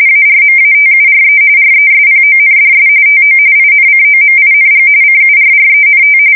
RTTY, about 45.45 baud. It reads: ARMATURE RESPONDS TO INCOMING CURRENT
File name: fsk170 / morkrum